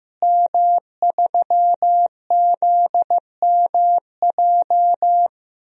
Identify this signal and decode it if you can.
Morse code, heard as a keyed tone, about 15 wpm, 700 Hz. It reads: M3ZMJ
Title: DXCC-100 Call Signs - 5 Characters - Repeated 3x - 15wpm